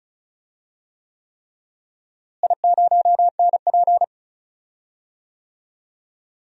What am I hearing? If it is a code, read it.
Morse code, heard as a keyed tone, about 36 wpm, 700 Hz. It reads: I0NP
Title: DXCC-100 Call Signs - 4 Characters - Repeated 3x - 35wpm